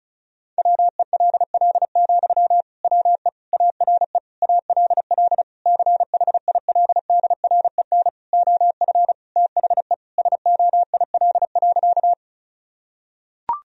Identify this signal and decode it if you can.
Morse code, heard as a keyed tone, about 36 wpm, 700 Hz. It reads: WELL, WE ARE ALL CHILDREN OF THE SOIL.
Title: Sentences from Top 1000 Words - Review Part 4 - 35wpm